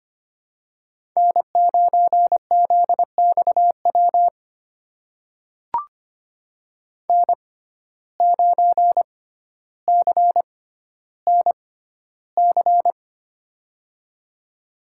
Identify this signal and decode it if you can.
Morse code, heard as a keyed tone, about 25 wpm, spaced as for 13 wpm, 700 Hz. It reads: N9ZXW N9CNC
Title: USA Call Signs - 5 Characters - 6x Character Spacing - 25wpm